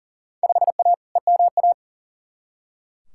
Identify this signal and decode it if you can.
Morse code, heard as a keyed tone, about 40 wpm, 700 Hz. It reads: 5A EMA
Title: Field Day Exchanges Encore - 40wpm